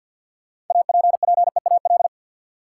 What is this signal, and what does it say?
Morse code, heard as a keyed tone, about 52 wpm, 700 Hz. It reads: APPEAL
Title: Sentences from Top 1300 Words - New Word Drill - 50wpm